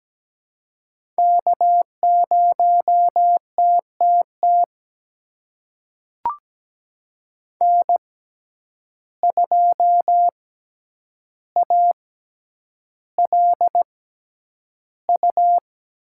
Morse code, heard as a keyed tone, about 17 wpm, spaced as for 9 wpm, 700 Hz. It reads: K0TTT N2ALU
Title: USA Call Signs - 5 Characters - 6x Character Spacing - 17wpm